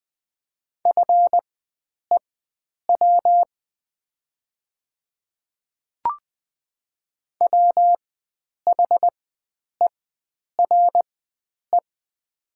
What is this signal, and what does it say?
Morse code, heard as a keyed tone, about 20 wpm, spaced as for 9 wpm, 700 Hz. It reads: FEW WHERE
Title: 4x Character Spacing - Top 300 Words - 20wpm